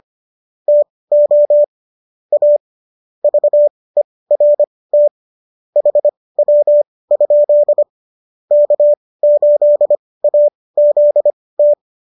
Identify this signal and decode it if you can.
Morse code, heard as a keyed tone, about 25 wpm, spaced as for 18 wpm, 590 Hz. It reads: TO A VERT HW? K8AZT